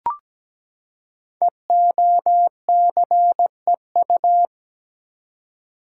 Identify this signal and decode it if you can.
Morse code, heard as a keyed tone, about 17 wpm, 700 Hz. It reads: EOCEU
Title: Code Groups - 5 Letter:Numbers - 17wpm